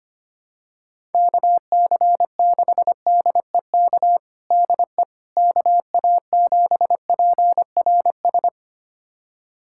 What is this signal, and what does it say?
Morse code, heard as a keyed tone, about 25 wpm, 700 Hz. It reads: KC6DEK DE KA7PRS
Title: USA Callsign DE Callsign - 6 Characters - 25wpm